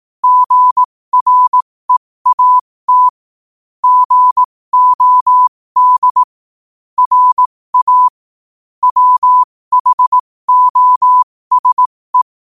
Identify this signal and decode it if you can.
Morse code, heard as a keyed tone, about 18 wpm, spaced as for 15 wpm, 1000 Hz. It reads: GREAT GOD RA WHOSE
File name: qotdfiles / 15wpm